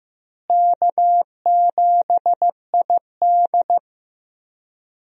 Morse code, heard as a keyed tone, about 15 wpm, 700 Hz. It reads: K7ID
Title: USA Call Signs - 4 Characters - Repeated 2x - 15wpm